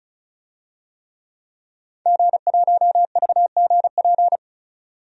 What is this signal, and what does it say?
Morse code, heard as a keyed tone, about 34 wpm, 700 Hz. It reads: G1VGP
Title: DXCC-100 Call Signs - 5 Characters - 35wpm